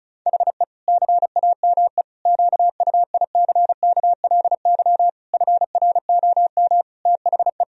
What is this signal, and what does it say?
Morse code, heard as a keyed tone, about 36 wpm, 700 Hz. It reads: HE CAME QUICKLY FROM THE
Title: Sentences from Top 1000 Words - Review Part 1 - 35wpm